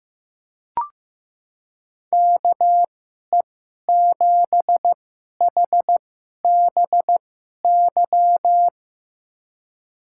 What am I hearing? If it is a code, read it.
Morse code, heard as a keyed tone, about 15 wpm, spaced as for 11 wpm, 700 Hz. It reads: KE7HBY
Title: USA Call Signs - 2x Character Spacing - 15wpm